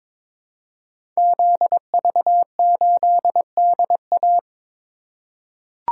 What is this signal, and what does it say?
Morse code, heard as a keyed tone, about 22 wpm, 700 Hz. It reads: ZV8DA
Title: Code Groups - 5 Letter:Numbers - 22wpm